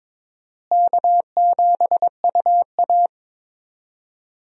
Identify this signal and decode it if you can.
Morse code, heard as a keyed tone, about 22 wpm, 700 Hz. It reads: K7UA
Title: USA Call Signs - 4 Characters - Repeated 3x - 22wpm